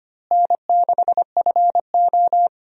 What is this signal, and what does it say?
Morse code, heard as a keyed tone, about 25 wpm, 700 Hz. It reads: N6FO
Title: USA Call Signs - 4 Characters - Repeated 2x - 25wpm